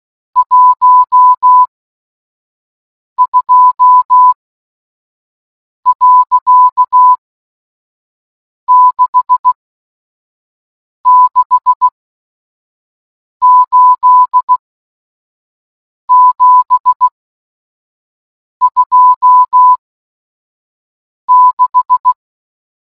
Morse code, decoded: 12.668726